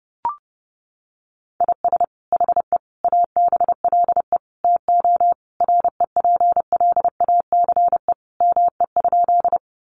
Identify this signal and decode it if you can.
Morse code, heard as a keyed tone, about 30 wpm, 700 Hz. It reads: IS HE ABLE TO REPLACE ME?
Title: Sentences from Top 1400 Words 30wpm